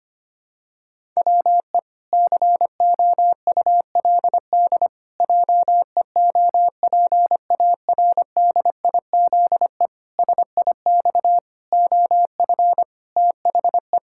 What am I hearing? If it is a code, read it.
Morse code, heard as a keyed tone, about 25 wpm, 700 Hz. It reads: WE COULD JEOPARDIZE SIX OF THE
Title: Pangram Challenge v2 - 25wpm